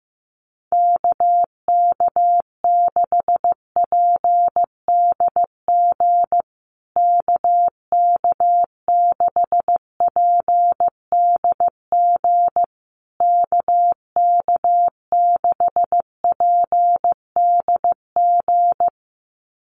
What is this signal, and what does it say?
Morse code, heard as a keyed tone, about 15 wpm, 700 Hz. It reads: KK6PDG KK6PDG KK6PDG
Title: USA Call Signs Repeated 3x 15wpm